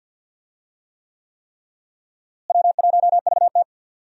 Morse code, heard as a keyed tone, about 51 wpm, 700 Hz. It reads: W1VT
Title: USA Call Signs - 4 Characters - Repeated 3x - 50wpm